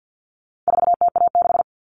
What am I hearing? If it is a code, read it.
Morse code, heard as a keyed tone, about 50 wpm, 700 Hz. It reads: 4TA6
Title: All Cut Numbers - 4-Digit 50wpm